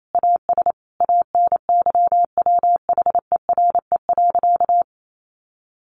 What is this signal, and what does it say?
Morse code, heard as a keyed tone, about 28 wpm, 700 Hz. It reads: AS ANYWHERE.
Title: Sentences from Top 1400 Words 28wpm